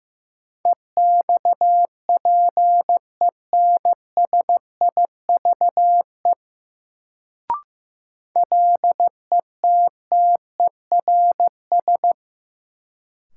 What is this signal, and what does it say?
Morse code, heard as a keyed tone, about 15 wpm, 700 Hz. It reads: EXPENSIVE LETTERS
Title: Sentences from Top 1000 Words - Review Part 6 - New Word Drill - 15wpm